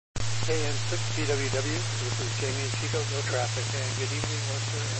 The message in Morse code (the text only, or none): none